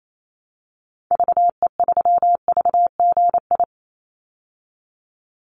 Morse code, heard as a keyed tone, about 28 wpm, 700 Hz. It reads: VE3VGI
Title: Call Signs Repeated 2x - 28wpm